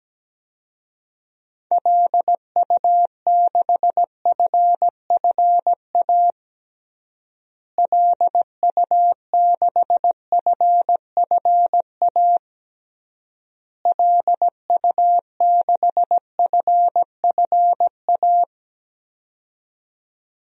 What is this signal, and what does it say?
Morse code, heard as a keyed tone, about 17 wpm, 700 Hz. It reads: LU6FFA LU6FFA LU6FFA
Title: DXCC-100 Call Signs - 6 Characters - Repeated 3x - 17wpm